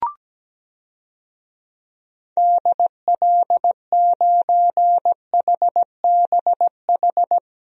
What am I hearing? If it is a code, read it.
Morse code, heard as a keyed tone, about 17 wpm, 700 Hz. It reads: DL9HBH
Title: DXCC-100 Call Signs - 6 Characters - 17wpm